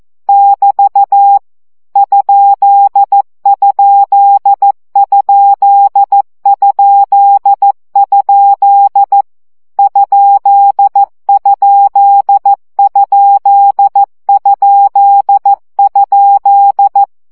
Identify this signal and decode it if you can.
Morse code, heard as a keyed tone, about 14 wpm, 800 Hz. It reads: <BT> ????? ?????
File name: CW gyakorlas kerdojel 12 baud